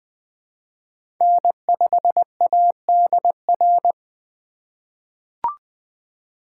Morse code, heard as a keyed tone, about 20 wpm, 700 Hz. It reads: N5ADR
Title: USA Call Signs - 5 Characters - 3x Character Spacing - 20wpm